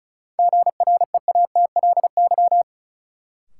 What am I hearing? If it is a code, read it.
Morse code, heard as a keyed tone, about 36 wpm, 700 Hz. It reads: GREATLY
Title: Sentences from Top 1400 Words - New Word Drill - 35wpm